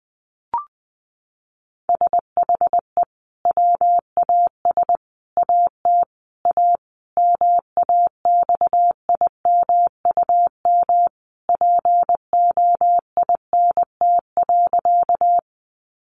Morse code, heard as a keyed tone, about 19 wpm, 700 Hz. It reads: SHE WAS AT A MAXIMUM POINT.
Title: Sentences from Top 1300 Words 20wpm